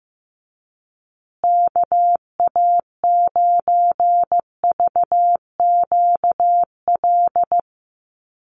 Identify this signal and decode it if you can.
Morse code, heard as a keyed tone, about 15 wpm, 700 Hz. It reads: KA9VQL
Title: USA Call Signs Repeated 3x 15wpm